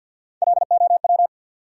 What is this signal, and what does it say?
Morse code, heard as a keyed tone, about 51 wpm, 700 Hz. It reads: LOW
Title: Three Letter Words - 50wpm